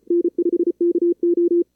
Morse code, heard as a keyed tone, about 34 wpm, 340 Hz. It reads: N5KO
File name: CWSprint / Feb2022 / k5tr-feb-na-sprint-cw-k5ot -2300